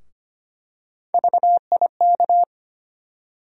Morse code, heard as a keyed tone, about 25 wpm, 700 Hz. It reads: VIK
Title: USA Call Sign Postfixes - 3 Letters 25wpm